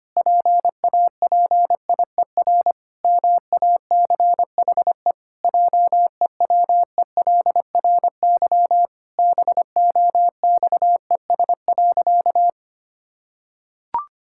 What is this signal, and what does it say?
Morse code, heard as a keyed tone, about 25 wpm, 700 Hz. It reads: PAPIER MACHE JEWELRY BOXES.
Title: Pangram Challenge v2 - 25wpm